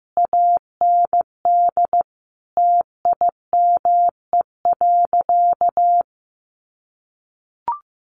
Morse code, heard as a keyed tone, about 15 wpm, 700 Hz. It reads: AND TIME.
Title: Sentences from Top 1300 Words 15wpm